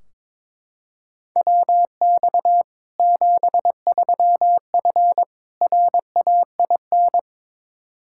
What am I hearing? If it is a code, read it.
Morse code, heard as a keyed tone, about 22 wpm, 700 Hz. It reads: WX 73F RAIN
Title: Practice QSO - Exchange 2 - Info - 22wpm